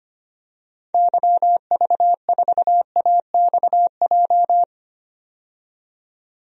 Morse code, heard as a keyed tone, about 25 wpm, 700 Hz. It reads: YV4AXJ
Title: DXCC-100 Call Signs - 6 Characters - 25wpm